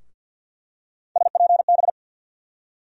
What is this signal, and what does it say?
Morse code, heard as a keyed tone, about 46 wpm, 700 Hz. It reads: SPB
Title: USA Call Sign Postfixes - 3 Letters 50wpm